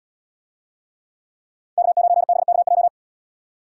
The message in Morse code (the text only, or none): K9BKJ